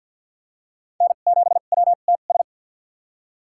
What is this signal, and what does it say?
Morse code, heard as a keyed tone, about 53 wpm, spaced as for 37 wpm, 700 Hz. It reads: N7WTS
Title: USA Call Signs - 5 Characters - 2x Character Spacing - 50wpm